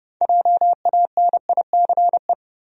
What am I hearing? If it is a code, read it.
Morse code, heard as a keyed tone, about 32 wpm, 700 Hz. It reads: JANICE
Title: USA Common Names - 30wpm